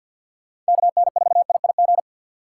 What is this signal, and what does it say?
Morse code, heard as a keyed tone, about 56 wpm, 700 Hz. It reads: KN4IIG
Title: Speed Racing - USA 2x3 Call Signs 50wpm